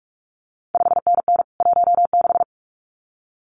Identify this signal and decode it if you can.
Morse code, heard as a keyed tone, about 46 wpm, 700 Hz. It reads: 5NN 16
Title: CQ WW DX Contest Exchange - 45wpm